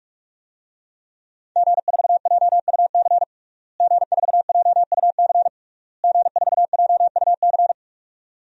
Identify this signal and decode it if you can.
Morse code, heard as a keyed tone, about 46 wpm, 700 Hz. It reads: G4JUC G4JUC G4JUC
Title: DXCC-100 Call Signs - 5 Characters - Repeated 3x - 45wpm